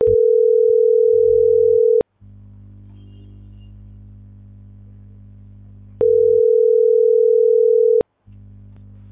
Ringback tone (North America), 440 + 480 Hz.